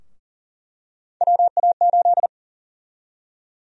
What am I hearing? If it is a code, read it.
Morse code, heard as a keyed tone, about 40 wpm, 700 Hz. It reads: WA8
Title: USA Call Sign Prefixes - Repeated 2x - 40wpm